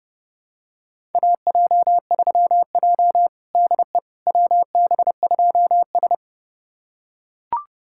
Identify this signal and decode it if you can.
Morse code, heard as a keyed tone, about 31 wpm, 700 Hz. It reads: AJ3J DE WB2S
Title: USA Callsign DE Callsign - 4 Characters - 30wpm